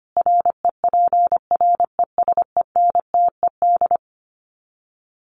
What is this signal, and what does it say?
Morse code, heard as a keyed tone, about 25 wpm, 700 Hz. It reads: REPRESENTED
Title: Sets of 5 Words 25wpm